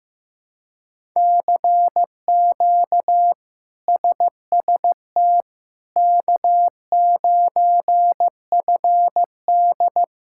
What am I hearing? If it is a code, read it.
Morse code, heard as a keyed tone, about 15 wpm, 700 Hz. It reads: CQ SST K9FD
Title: K1USN SST QSO Practice - 15wpm